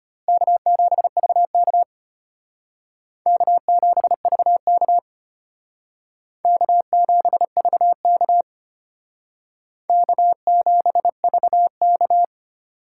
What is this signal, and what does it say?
Morse code, heard as a keyed tone, about 33 wpm, 700 Hz. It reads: K7VK K7VK K7VK K7VK